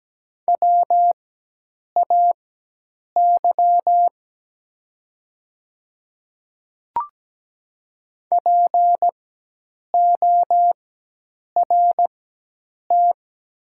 Morse code, heard as a keyed tone, about 17 wpm, spaced as for 8 wpm, 700 Hz. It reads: WAY PORT
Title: 4x Character Spacing - Top 300 Words - 17wpm